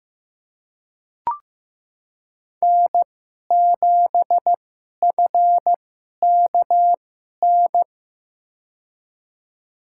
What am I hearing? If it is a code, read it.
Morse code, heard as a keyed tone, about 15 wpm, spaced as for 11 wpm, 700 Hz. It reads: N7FKN
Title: USA Call Signs - 5 Characters - 2x Character Spacing - 15wpm